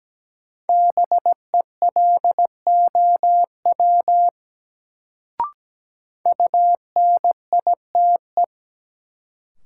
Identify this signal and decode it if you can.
Morse code, heard as a keyed tone, about 17 wpm, 700 Hz. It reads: BELOW UNITE
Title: Sentences from Top 1000 Words - Review Part 5 - New Word Drill - 17wpm